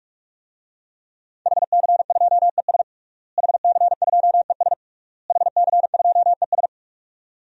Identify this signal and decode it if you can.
Morse code, heard as a keyed tone, about 41 wpm, 700 Hz. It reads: HC2ES HC2ES HC2ES
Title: DXCC-100 Call Signs - 5 Characters - Repeated 3x - 45wpm